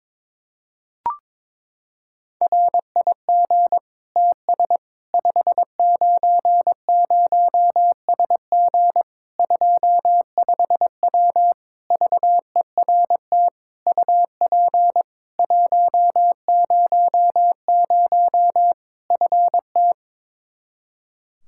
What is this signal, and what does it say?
Morse code, heard as a keyed tone, about 22 wpm, 700 Hz. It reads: RIG TS 590SG 25W VERT UP 100 FT
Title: Practice QSO - Exchange 2 - Info - 22wpm